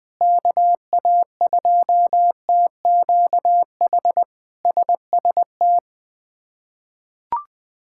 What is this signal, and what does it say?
Morse code, heard as a keyed tone, about 20 wpm, 700 Hz. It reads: KA2TQH SST
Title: K1USN SST QSO Practice - 20wpm